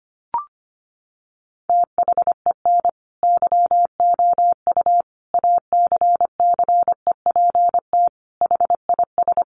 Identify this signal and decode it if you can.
Morse code, heard as a keyed tone, about 25 wpm, 700 Hz. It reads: THEN YOU ACCEPT HIS
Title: Sentences from Top 1300 Words 25wpm